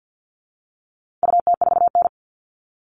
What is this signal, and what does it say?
Morse code, heard as a keyed tone, about 50 wpm, 700 Hz. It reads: UT4N
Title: All Cut Numbers - 4-Digit 50wpm